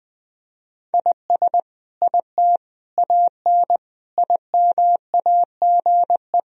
Morse code, heard as a keyed tone, about 20 wpm, 700 Hz. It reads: IS IT AN IMAGE
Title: Sentences from Top 1000 Words - Review Part 4 - 20wpm